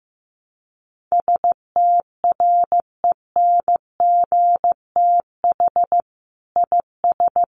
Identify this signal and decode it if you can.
Morse code, heard as a keyed tone, about 15 wpm, 700 Hz. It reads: STRENGTH IS
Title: Sentences from Top 1300 Words 15wpm